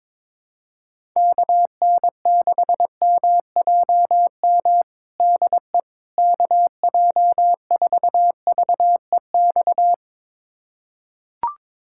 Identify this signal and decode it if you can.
Morse code, heard as a keyed tone, about 22 wpm, 700 Hz. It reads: KN6MJM DE KJ4VEX
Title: USA Callsign DE Callsign - 6 Characters - 22wpm